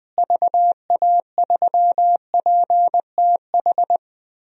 Morse code, heard as a keyed tone, about 20 wpm, 700 Hz. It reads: VA3PTH
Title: DXCC-100 Call Signs - 6 Characters - Repeated 2x - 20wpm